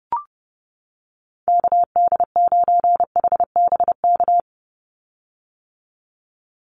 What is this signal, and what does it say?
Morse code, heard as a keyed tone, about 30 wpm, 700 Hz. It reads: KD9HBK
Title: USA Call Signs 30wpm